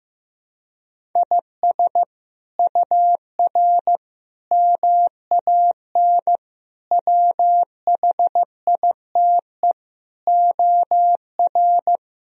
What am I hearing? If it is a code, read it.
Morse code, heard as a keyed tone, about 15 wpm, 700 Hz. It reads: IS UR MAN WHITE OR
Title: Sentences from Top 400 Words and Abbreviations 15wpm